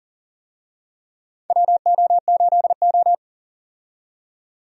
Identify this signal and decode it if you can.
Morse code, heard as a keyed tone, about 40 wpm, 700 Hz. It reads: WO8O
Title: USA Call Signs - 4 Characters - 40wpm